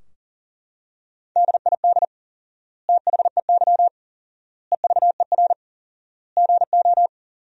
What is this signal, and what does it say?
Morse code, heard as a keyed tone, about 37 wpm, 700 Hz. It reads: DID THEY EVER GO
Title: 4x Word-Spacing - Sentences from Top 400 Words - 40wpm